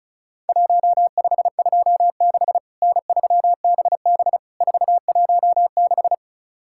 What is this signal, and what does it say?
Morse code, heard as a keyed tone, about 36 wpm, 700 Hz. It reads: 1526 N3BB 416
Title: Worked All Europe DX Contest - QTC Exchange - 35wpm